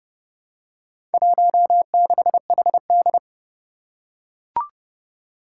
Morse code, heard as a keyed tone, about 30 wpm, 700 Hz. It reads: 16HD
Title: Four Challenging Letter-Number Combinations 30wpm